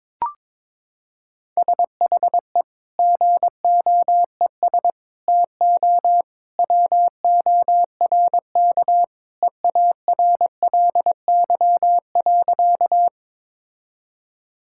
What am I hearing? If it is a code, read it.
Morse code, heard as a keyed tone, about 22 wpm, 700 Hz. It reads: SHE GOES TO WORK EARLY.
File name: practice / 22wpm / Sentences with 2-Word Phrases - 22wpm